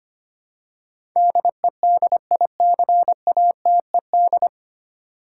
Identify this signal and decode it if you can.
Morse code, heard as a keyed tone, about 25 wpm, 700 Hz. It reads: DEDICATED
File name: practice / 25wpm / 9 Letter Words - First 3 Letters Given - 25wpm